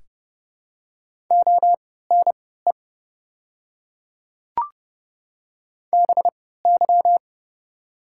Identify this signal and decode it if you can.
Morse code, heard as a keyed tone, about 32 wpm, spaced as for 17 wpm, 700 Hz. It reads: ONE BY
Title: 3x Character Spacing - Top 100 Words - 30wpm